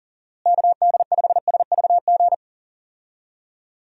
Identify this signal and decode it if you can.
Morse code, heard as a keyed tone, about 38 wpm, 700 Hz. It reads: KD5SVG
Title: USA Call Signs - 6 Characters - Repeated 2x - 40wpm